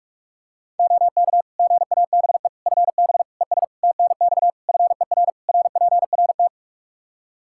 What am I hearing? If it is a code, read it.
Morse code, heard as a keyed tone, about 41 wpm, 700 Hz. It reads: OK GABE FB ES TNX FER RPRT
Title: Practice QSO - Exchange 2 - Info - 45wpm